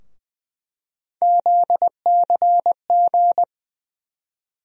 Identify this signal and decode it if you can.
Morse code, heard as a keyed tone, about 20 wpm, 700 Hz. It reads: ZCG
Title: USA Call Sign Postfixes - 3 Letters 20wpm